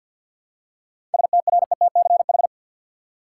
Morse code, heard as a keyed tone, about 53 wpm, 700 Hz. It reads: STRETCH
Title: Seven Letter Words - 50wpm